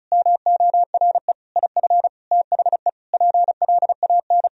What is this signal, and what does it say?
Morse code, heard as a keyed tone, about 34 wpm, 700 Hz. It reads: MORE IF THE PLAN
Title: Sentences from Top 1000 Words - Review Part 2 - 35wpm